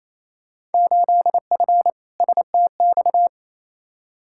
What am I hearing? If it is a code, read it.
Morse code, heard as a keyed tone, about 27 wpm, 700 Hz. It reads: 8F STX
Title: Field Day Exchanges Encore - 28wpm